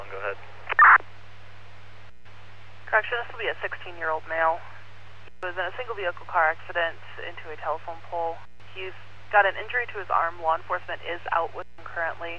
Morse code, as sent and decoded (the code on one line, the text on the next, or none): none